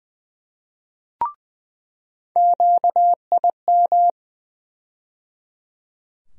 Morse code, heard as a keyed tone, about 20 wpm, 700 Hz. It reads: QIM
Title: USA Call Sign Postfixes - 3 Letters 20wpm